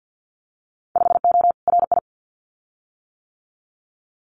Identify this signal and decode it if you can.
Morse code, heard as a keyed tone, about 50 wpm, 700 Hz. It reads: 5O RI
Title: Winter Field Day Exchanges Encore 50wpm